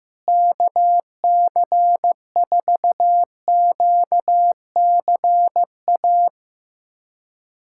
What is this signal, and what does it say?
Morse code, heard as a keyed tone, about 15 wpm, 700 Hz. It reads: KC4QCA
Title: USA Callsign DE Callsign 15wpm